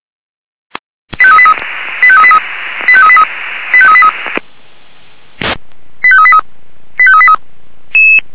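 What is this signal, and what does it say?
Five-tone selective call (ZVEI), heard as quick runs of five tones, tones of about 70 ms. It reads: 85383 85383 85383 85383 85383 85383
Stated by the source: ZVEI five-tone selective call 85383 received on an SDR.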